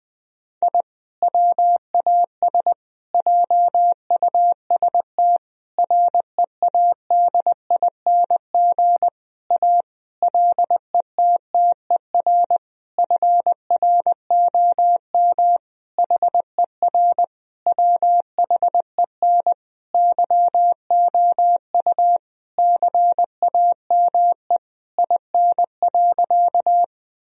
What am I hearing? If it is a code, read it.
Morse code, heard as a keyed tone, about 20 wpm, 700 Hz. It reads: I WAS JUST READING A LETTER FROM HER WHEN YOU CAME IN.
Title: Sentences from Top 1000 Words - Review Part 3 - 20wpm